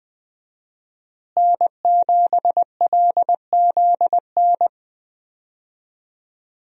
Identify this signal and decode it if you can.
Morse code, heard as a keyed tone, about 20 wpm, 700 Hz. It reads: N7LZN
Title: USA Call Signs - 5 Characters - Repeated 2x - 20wpm